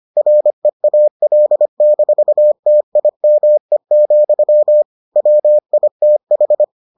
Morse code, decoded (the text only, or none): REAL-TIME, WITH